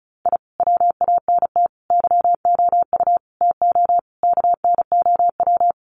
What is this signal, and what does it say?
Morse code, heard as a keyed tone, about 34 wpm, 700 Hz. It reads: I WANT YOU TO KNOW
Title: Sets of 5 Words Encore 35wpm